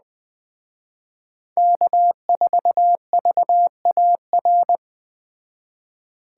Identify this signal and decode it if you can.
Morse code, heard as a keyed tone, about 19 wpm, 700 Hz. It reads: K4VAR